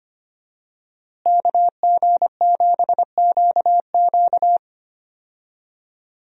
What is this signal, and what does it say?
Morse code, heard as a keyed tone, about 25 wpm, 700 Hz. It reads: KG7QQ
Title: USA Call Signs - 5 Characters - Repeated 2x - 25wpm